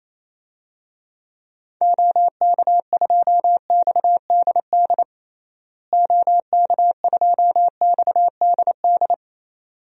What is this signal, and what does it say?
Morse code, heard as a keyed tone, about 28 wpm, 700 Hz. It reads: OK2XDD OK2XDD
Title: DXCC-100 Call Signs - 6 Characters - Repeated 2x - 28wpm